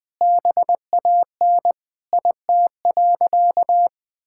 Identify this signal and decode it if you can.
Morse code, heard as a keyed tone, about 21 wpm, 700 Hz. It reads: BAN IT.
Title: Sentences with 8-Word Phrases - 20wpm